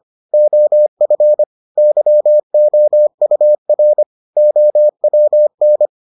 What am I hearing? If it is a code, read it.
Morse code, heard as a keyed tone, about 25 wpm, 600 Hz. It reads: OF YOUR OWN